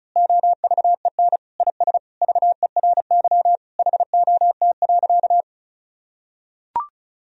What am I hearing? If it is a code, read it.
Morse code, heard as a keyed tone, about 36 wpm, 700 Hz. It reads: OVEN IS VERY HOT.
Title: Sentences from Top 1000 Words - Review Part 4 - 35wpm